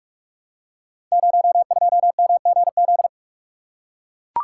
Morse code, heard as a keyed tone, about 44 wpm, 700 Hz. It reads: 02MGZ
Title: Code Groups - 5 Letter:Numbers - 45wpm